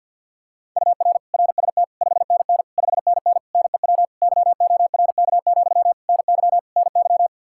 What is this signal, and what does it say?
Morse code, heard as a keyed tone, about 52 wpm, 700 Hz. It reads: UR RST 5NN 5NN NEW YORK, NY NY
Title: Practice QSO - Exchange 1 - Beginning - 50wpm